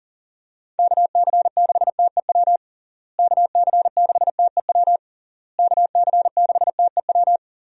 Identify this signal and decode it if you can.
Morse code, heard as a keyed tone, about 40 wpm, 700 Hz. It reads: KC6TEW KC6TEW KC6TEW
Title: USA Call Signs - 6 Characters - Repeated 3x - 40wpm